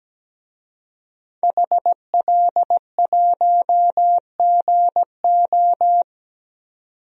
Morse code, heard as a keyed tone, about 17 wpm, 700 Hz. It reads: HL1GO
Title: DXCC-100 Call Signs - 5 Characters - 17wpm